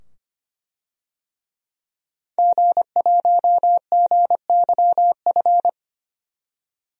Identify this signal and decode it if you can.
Morse code, heard as a keyed tone, about 25 wpm, 700 Hz. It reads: G1GYF
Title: DXCC-100 Call Signs - 5 Characters - 25wpm